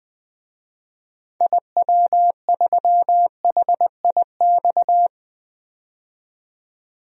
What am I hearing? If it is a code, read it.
Morse code, heard as a keyed tone, about 20 wpm, 700 Hz. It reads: IW3HIX
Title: DXCC-100 Call Signs - 6 Characters - 20wpm